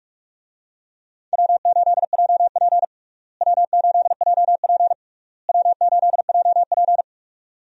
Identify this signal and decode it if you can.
Morse code, heard as a keyed tone, about 46 wpm, 700 Hz. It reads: W8JP W8JP W8JP
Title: USA Call Signs - 4 Characters - Repeated 3x - 45wpm